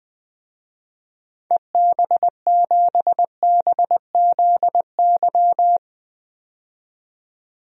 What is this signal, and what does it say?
Morse code, heard as a keyed tone, about 20 wpm, 700 Hz. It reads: EB7BZY
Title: DXCC-100 Call Signs - 6 Characters - 20wpm